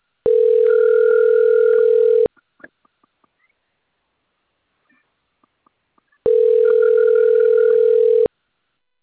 Ringback tone (North America), 440 + 480 Hz.
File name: rg-1600-unknown- - - .58302